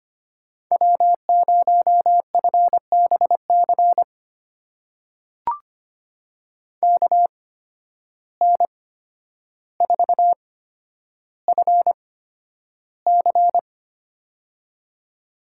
Morse code, heard as a keyed tone, about 25 wpm, spaced as for 11 wpm, 700 Hz. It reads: W0FBC KN4FC
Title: USA Call Signs - 5 Characters - 8x Character Spacing - 25wpm